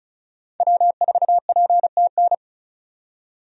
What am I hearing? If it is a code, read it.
Morse code, heard as a keyed tone, about 36 wpm, 700 Hz. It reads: W4PTN
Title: USA Call Signs - 5 Characters - 35wpm